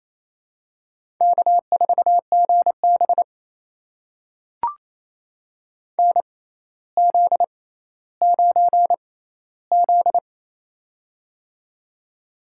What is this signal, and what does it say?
Morse code, heard as a keyed tone, about 28 wpm, spaced as for 14 wpm, 700 Hz. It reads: K4GB NZ9Z
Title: USA Call Signs - 4 Characters - 6x Character Spacing - 28wpm